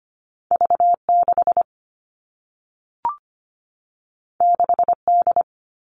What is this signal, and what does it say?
Morse code, heard as a keyed tone, about 25 wpm, 700 Hz. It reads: V 6 6 D
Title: Two Challenging Letter-Number Combinations 25wpm